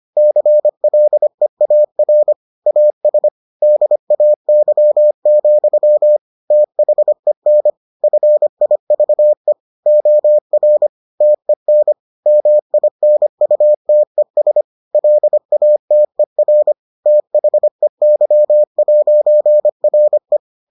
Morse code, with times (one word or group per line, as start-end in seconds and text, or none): 0.2-2.3 CLEAR
2.7-3.3 AS
3.6-6.2 DAY,
6.5-7.7 THEN
8.0-9.5 FIVE
9.9-10.9 OR
11.2-11.9 TEN
12.3-14.6 MINUTES
14.9-16.7 LATER
17.1-20.4 THEY'RE